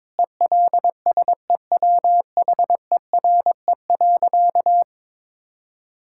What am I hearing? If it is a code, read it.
Morse code, heard as a keyed tone, about 21 wpm, 700 Hz. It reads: ELSEWHERE.
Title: Sentences from Top 1600 Words 22wpm